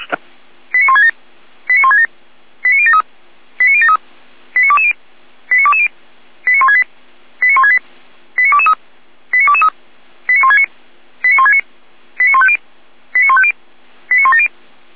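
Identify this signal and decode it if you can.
Five-tone selective call (ZVEI), heard as quick runs of five tones, tones of about 70 ms. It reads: 89167 89167 89073 89073 89220 89220 89168 89168 89223 89223 89169 89169 89150 89150 89170
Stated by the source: ZVEI five-tone selective call 89022, received via SDR.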